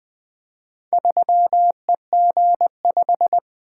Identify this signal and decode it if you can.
Morse code, heard as a keyed tone, about 20 wpm, 700 Hz. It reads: 3EG5
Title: Four Letter-Number Combinations - 20wpm